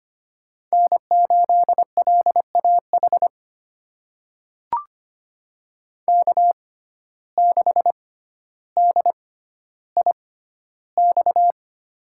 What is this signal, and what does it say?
Morse code, heard as a keyed tone, about 25 wpm, spaced as for 13 wpm, 700 Hz. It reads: N8LAH K6DIX
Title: USA Call Signs - 5 Characters - 6x Character Spacing - 25wpm